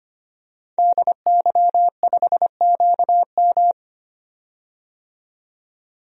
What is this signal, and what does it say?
Morse code, heard as a keyed tone, about 25 wpm, 700 Hz. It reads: DY5QM
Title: DXCC-100 Call Signs - 5 Characters - 25wpm